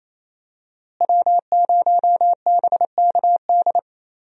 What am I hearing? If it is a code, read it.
Morse code, heard as a keyed tone, about 28 wpm, 700 Hz. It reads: W0BKD
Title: USA Call Signs - 5 Characters - Repeated 2x - 28wpm